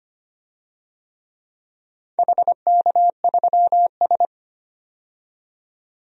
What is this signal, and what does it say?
Morse code, heard as a keyed tone, about 25 wpm, 700 Hz. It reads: HK3S
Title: DXCC-100 Call Signs - 4 Characters - 25wpm